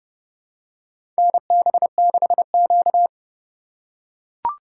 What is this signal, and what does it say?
Morse code, heard as a keyed tone, about 30 wpm, 700 Hz. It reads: NB6Q
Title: USA Call Signs - 4 Characters - 4x Character Spacing - 30wpm